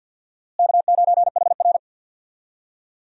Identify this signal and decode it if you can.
Morse code, heard as a keyed tone, about 52 wpm, 700 Hz. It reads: K9HR
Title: USA Call Signs - 4 Characters - Repeated 3x - 50wpm